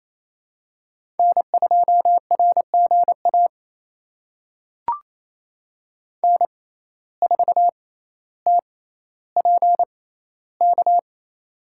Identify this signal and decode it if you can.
Morse code, heard as a keyed tone, about 28 wpm, spaced as for 14 wpm, 700 Hz. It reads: N2RGA N4TPK
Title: USA Call Signs - 5 Characters - 6x Character Spacing - 28wpm